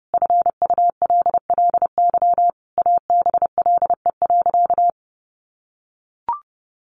Morse code, heard as a keyed tone, about 30 wpm, 700 Hz. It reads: FULLY ABLE.
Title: Sentences from Top 1300 Words 30wpm